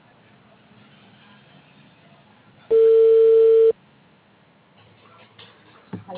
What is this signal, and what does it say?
Ringback tone (China, 450 Hz), with speech.